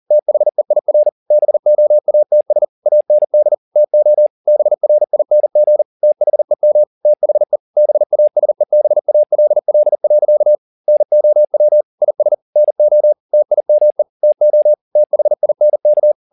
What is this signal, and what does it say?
Morse code, heard as a keyed tone, about 40 wpm, 600 Hz. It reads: THEIR BOATS AND TO BRING THEM THE BASEBALL. NOW IS NO TIME TO THINK